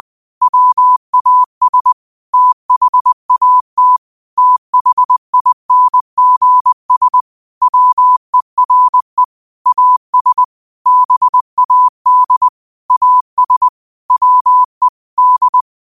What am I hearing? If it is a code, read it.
Morse code, heard as a keyed tone, about 20 wpm, 1000 Hz. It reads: WAS THAT THINGS WERE AS BAD AS WE D
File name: qotdfiles / 20wpm